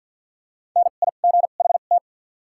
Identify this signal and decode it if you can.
Morse code, heard as a keyed tone, about 53 wpm, spaced as for 37 wpm, 700 Hz. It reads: NIGHT
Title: 2x Character Spacing - Top 300 Words - 50wpm